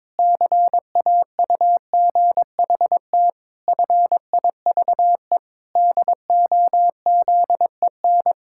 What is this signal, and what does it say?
Morse code, heard as a keyed tone, about 22 wpm, 700 Hz. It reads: CAUGHT FIVE DOZEN
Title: Pangram Challenge v2 - 22wpm